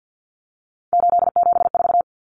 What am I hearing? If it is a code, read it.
Morse code, heard as a keyed tone, about 50 wpm, 700 Hz. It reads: 874
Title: Numbers - 3-Digit 50wpm